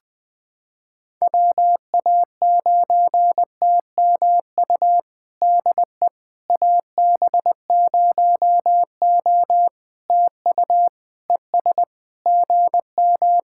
Morse code, heard as a keyed tone, about 20 wpm, 700 Hz. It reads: WA9TMU DE AB0O TU ES GM
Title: POTA QSO Practice - 20wpm